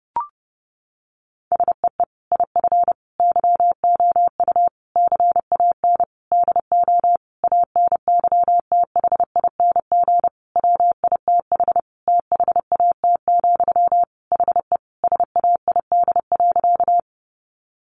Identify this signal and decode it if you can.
Morse code, heard as a keyed tone, about 30 wpm, 700 Hz. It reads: SEE IF YOU CAN DO ANYTHING WITH THAT, HE SAID.
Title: Sentences from Top 1000 Words 30wpm